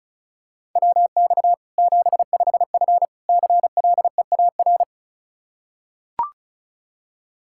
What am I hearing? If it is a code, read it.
Morse code, heard as a keyed tone, about 36 wpm, 700 Hz. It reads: WX 75F CLEAR
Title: Practice QSO - Exchange 2 - Info - 35wpm